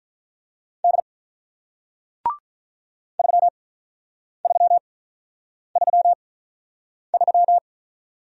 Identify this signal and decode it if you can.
Morse code, heard as a keyed tone, about 46 wpm, spaced as for 9 wpm, 700 Hz. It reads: D 3333